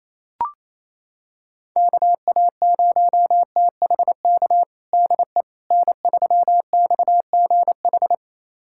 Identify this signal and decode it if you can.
Morse code, heard as a keyed tone, about 27 wpm, 700 Hz. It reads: KA0THK DE N3XGH
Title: USA Callsign DE Callsign 28wpm